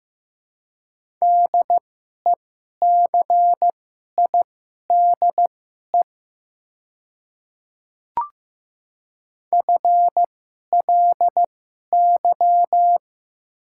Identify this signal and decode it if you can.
Morse code, heard as a keyed tone, about 15 wpm, spaced as for 11 wpm, 700 Hz. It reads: DECIDE FLY
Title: 2x Character Spacing - Top 500 Words - 15wpm